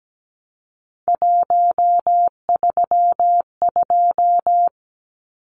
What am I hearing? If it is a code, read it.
Morse code, heard as a keyed tone, about 17 wpm, 700 Hz. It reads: 132
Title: Numbers - 3-Digit 17wpm